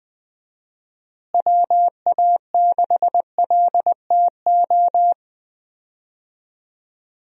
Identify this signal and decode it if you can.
Morse code, heard as a keyed tone, about 20 wpm, 700 Hz. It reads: WA6LTO
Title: USA Call Signs - 6 Characters - Repeated 2x - 20wpm